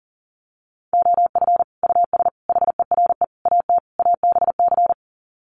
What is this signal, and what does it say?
Morse code, heard as a keyed tone, about 40 wpm, 700 Hz. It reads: OF US HERE AT ABC
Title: Sets of 5 Words 40wpm